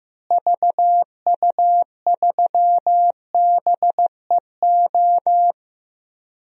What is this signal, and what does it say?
Morse code, heard as a keyed tone, about 15 wpm, 700 Hz. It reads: VU3BEO
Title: DXCC-100 Call Signs - 6 Characters - Repeated 2x - 15wpm